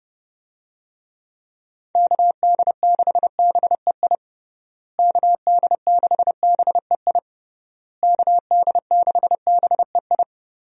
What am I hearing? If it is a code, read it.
Morse code, heard as a keyed tone, about 30 wpm, 700 Hz. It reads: KD6BEI KD6BEI KD6BEI
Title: USA Call Signs - 6 Characters - Repeated 3x - 30wpm